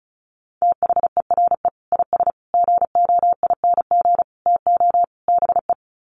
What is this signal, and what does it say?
Morse code, heard as a keyed tone, about 34 wpm, 700 Hz. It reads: THERE IS GOING TO BE
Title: Sets of 5 Words Encore 35wpm